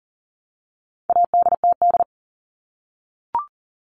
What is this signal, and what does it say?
Morse code, heard as a keyed tone, about 40 wpm, 700 Hz. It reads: ADTD
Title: All Cut Numbers - 4-Digit 40wpm